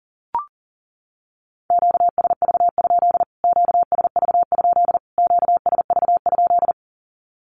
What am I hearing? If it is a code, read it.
Morse code, heard as a keyed tone, about 40 wpm, 700 Hz. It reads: QSV? QSV? QSV?
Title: Q-Codes Repeated 3x 40wpm